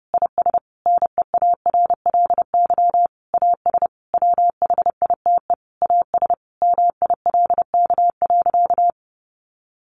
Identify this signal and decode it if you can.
Morse code, heard as a keyed tone, about 30 wpm, 700 Hz. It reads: IS NEARLY AS WHITE AS MILK.
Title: Sentences from Top 1500 Words 30wpm